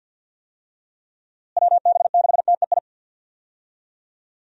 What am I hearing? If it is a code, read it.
Morse code, heard as a keyed tone, about 52 wpm, 700 Hz. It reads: WB6TEI
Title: USA Call Signs - 6 Characters - Repeated 2x - 50wpm